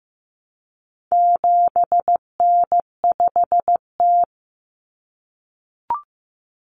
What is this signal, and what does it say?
Morse code, heard as a keyed tone, about 15 wpm, 700 Hz. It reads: 7N5T
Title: Common Cut Numbers - 4-Digit 15wpm